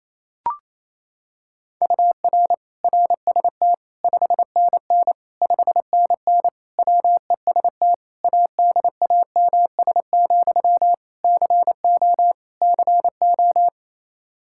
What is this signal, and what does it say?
Morse code, heard as a keyed tone, about 28 wpm, 700 Hz. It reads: UR RST 5NN 5NN WEST ADAMS, CO CO
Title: Practice QSO - Exchange 1 - Beginning - 28wpm